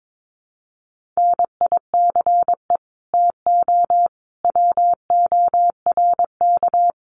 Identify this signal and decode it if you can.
Morse code, heard as a keyed tone, about 22 wpm, 700 Hz. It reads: NICE TO WORK
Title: Common QSO Phrases 22wpm